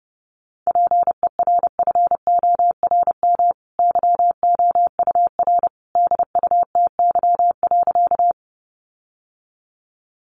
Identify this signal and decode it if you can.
Morse code, heard as a keyed tone, about 30 wpm, 700 Hz. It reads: PERFORM YOUR DUTY.
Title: Sentences from Top 1500 Words 30wpm